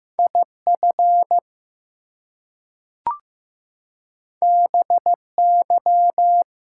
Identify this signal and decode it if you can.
Morse code, heard as a keyed tone, about 15 wpm, 700 Hz. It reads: IF BY
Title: Two Letter Words - 15wpm